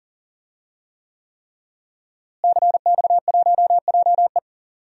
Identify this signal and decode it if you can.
Morse code, heard as a keyed tone, about 39 wpm, 700 Hz. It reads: CX1JE
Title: DXCC-100 Call Signs - 5 Characters - Repeated 3x - 40wpm